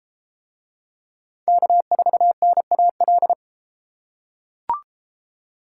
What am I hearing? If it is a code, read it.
Morse code, heard as a keyed tone, about 32 wpm, 700 Hz. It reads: K4NAL